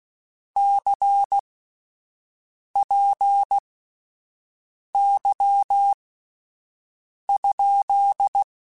Morse code, heard as a keyed tone, about 16 wpm, spaced as for 5 wpm, 800 Hz. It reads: CPY?
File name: (spaced) / QSOs Spaced-mp3s / WA7QOX 1 @10 sp